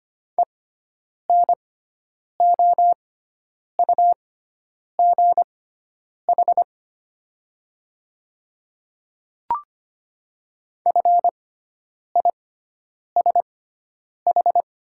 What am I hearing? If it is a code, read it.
Morse code, heard as a keyed tone, about 25 wpm, spaced as for 9 wpm, 690 Hz. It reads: ENOUGH FISH
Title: 6x Character Spacing - Top 500 Words - 25wpm